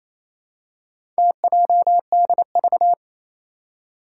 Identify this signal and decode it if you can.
Morse code, heard as a keyed tone, about 29 wpm, 700 Hz. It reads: TJDV
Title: Four Challenging Letter-Number Combinations 28wpm